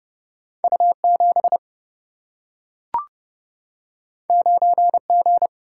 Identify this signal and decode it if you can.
Morse code, heard as a keyed tone, about 30 wpm, 700 Hz. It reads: U 7 9 G
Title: Single-Letter Single-Number 30wpm